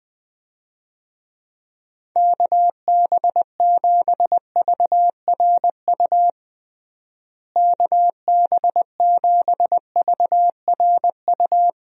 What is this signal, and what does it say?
Morse code, heard as a keyed tone, about 20 wpm, 700 Hz. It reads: KB7VRU KB7VRU
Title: USA Call Signs - 6 Characters - Repeated 2x - 20wpm